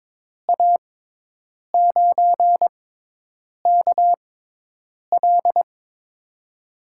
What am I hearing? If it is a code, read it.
Morse code, heard as a keyed tone, about 22 wpm, spaced as for 8 wpm, 700 Hz. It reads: A9KL
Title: USA Call Signs - 5 Characters - 6x Character Spacing - 22wpm